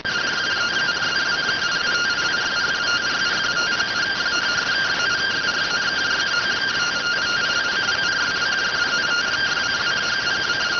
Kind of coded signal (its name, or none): two-tone teleprinter signal